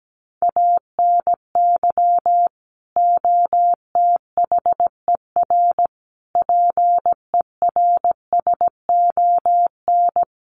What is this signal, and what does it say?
Morse code, heard as a keyed tone, about 17 wpm, 700 Hz. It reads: ANY OTHER PERSON
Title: Sentences from Top 1100 Words 17wpm